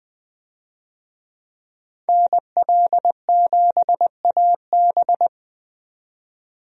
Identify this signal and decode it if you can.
Morse code, heard as a keyed tone, about 20 wpm, 700 Hz. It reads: NL7AB
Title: USA Call Signs - 5 Characters - 20wpm